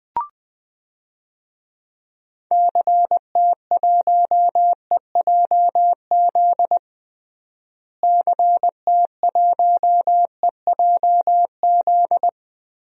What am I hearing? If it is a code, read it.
Morse code, heard as a keyed tone, about 20 wpm, 700 Hz. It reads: CT1EJZ CT1EJZ
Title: DXCC-100 Call Signs - 6 Characters - Repeated 2x - 20wpm